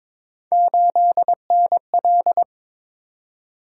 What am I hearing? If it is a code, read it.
Morse code, heard as a keyed tone, about 22 wpm, 700 Hz. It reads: 8NL
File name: practice / 22wpm / Three Letter-Number Combinations - 22wpm